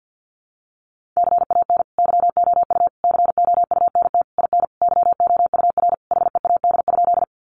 Morse code, heard as a keyed tone, about 50 wpm, 700 Hz. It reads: CAN YOU COUNT IN YOUR HEAD?
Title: Sentences from Top 1500 Words 50wpm